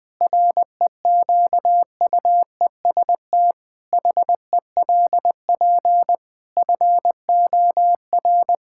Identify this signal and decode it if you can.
Morse code, heard as a keyed tone, about 20 wpm, 700 Hz. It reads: REQUEST HELP FOR
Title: Pangram Challenge v2 - 20wpm